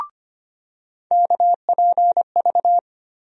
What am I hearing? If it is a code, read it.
Morse code, heard as a keyed tone, about 25 wpm, 700 Hz. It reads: KPV
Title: USA Call Sign Postfixes - 3 Letters 25wpm